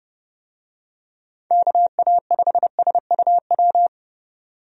Morse code, heard as a keyed tone, about 30 wpm, 700 Hz. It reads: KA5SUW
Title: USA Call Signs - 6 Characters - 30wpm